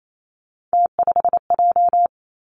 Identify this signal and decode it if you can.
Morse code, heard as a keyed tone, about 28 wpm, 700 Hz. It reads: T5J
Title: Three Challenging Letter-Number Combinations 28wpm